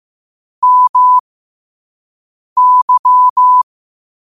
Morse code, heard as a keyed tone, about 15 wpm, spaced as for 5 wpm, 1000 Hz. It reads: MY